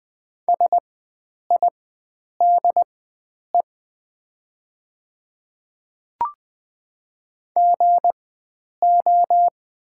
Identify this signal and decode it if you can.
Morse code, heard as a keyed tone, about 20 wpm, spaced as for 9 wpm, 700 Hz. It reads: SIDE GO
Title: 4x Character Spacing - Top 100 Words - 20wpm